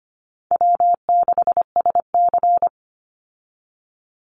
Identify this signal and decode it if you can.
Morse code, heard as a keyed tone, about 25 wpm, 700 Hz. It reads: W6SC
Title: Easy Call Signs Repeated 2x - 25wpm